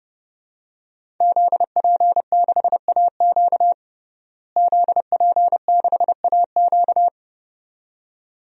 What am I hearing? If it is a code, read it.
Morse code, heard as a keyed tone, about 30 wpm, 700 Hz. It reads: ZP6AQ ZP6AQ
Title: DXCC-100 Call Signs - 5 Characters - Repeated 2x - 30wpm